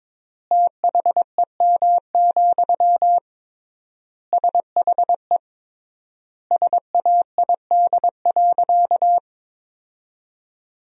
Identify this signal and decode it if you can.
Morse code, heard as a keyed tone, about 22 wpm, 700 Hz. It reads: THEM, SHE SAID.
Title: 3x Word-Spacing - Sentences from Top 400 Words - 22wpm